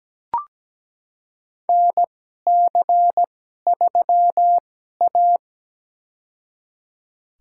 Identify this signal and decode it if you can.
Morse code, heard as a keyed tone, about 17 wpm, spaced as for 12 wpm, 700 Hz. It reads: NC3A
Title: USA Call Signs - 2x Character Spacing - 17wpm